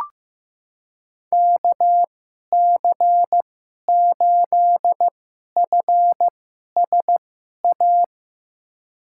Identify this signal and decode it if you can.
Morse code, heard as a keyed tone, about 16 wpm, spaced as for 11 wpm, 700 Hz. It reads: KC8FSA
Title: USA Call Signs - 2x Character Spacing - 15wpm